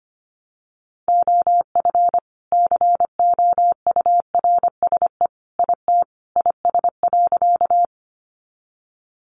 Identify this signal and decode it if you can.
Morse code, heard as a keyed tone, about 25 wpm, 700 Hz. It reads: OF COURSE IT IS.
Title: Sentences from Top 500 Words 25wpm